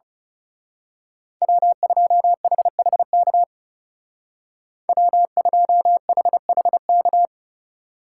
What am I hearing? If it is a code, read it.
Morse code, heard as a keyed tone, about 34 wpm, 700 Hz. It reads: W2HHK W2HHK
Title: Speed Racing - USA 1x3 Call Signs 30wpm